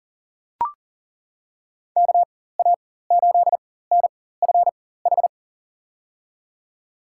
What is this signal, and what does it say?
Morse code, heard as a keyed tone, about 40 wpm, spaced as for 19 wpm, 700 Hz. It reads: KA8NFH
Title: USA Call Signs - 4x Character Spacing - 40wpm